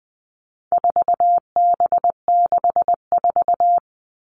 Morse code, heard as a keyed tone, about 20 wpm, 700 Hz. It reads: 4B64
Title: All Cut Numbers - 4-Digit 20wpm